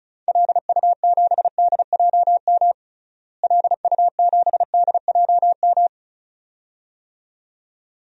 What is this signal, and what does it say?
Morse code, heard as a keyed tone, about 36 wpm, 700 Hz. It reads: LU7DJM LU7DJM
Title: DXCC-100 Call Signs - 6 Characters - Repeated 2x - 35wpm